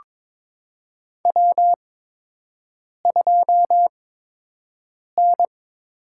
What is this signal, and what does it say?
Morse code, heard as a keyed tone, about 22 wpm, spaced as for 6 wpm, 700 Hz. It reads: W2N
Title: USA Call Signs - 5 Characters - 8x Character Spacing - 22wpm